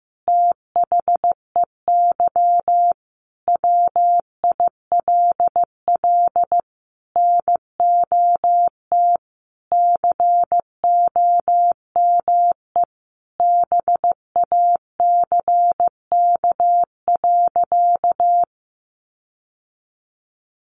Morse code, heard as a keyed tone, about 15 wpm, 700 Hz. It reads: THEY WILL NOT COME BACK.
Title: Sentences from Top 1300 Words 15wpm